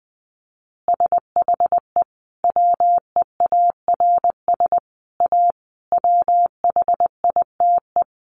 Morse code, heard as a keyed tone, about 20 wpm, 700 Hz. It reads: SHE WEARS A WHITE
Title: Additional Easy Phrases 20wpm